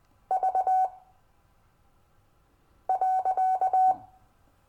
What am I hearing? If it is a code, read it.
Morse code, heard as a keyed tone, about 20 wpm, spaced as for 4 wpm, 750 Hz. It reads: V.
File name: cw / Lesson15 five forwardslash